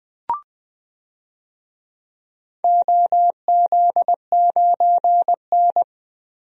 Morse code, heard as a keyed tone, about 21 wpm, 700 Hz. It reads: OZ9N
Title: DXCC-100 Call Signs - 4 Characters - 20wpm